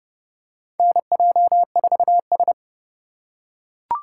Morse code, heard as a keyed tone, about 32 wpm, 700 Hz. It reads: NJ4S
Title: USA Call Signs - 4 Characters - 4x Character Spacing - 30wpm